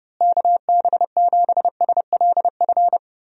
Morse code, heard as a keyed tone, about 30 wpm, 700 Hz. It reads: KB7SLF
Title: USA Call Signs - 6 Characters - Repeated 2x - 30wpm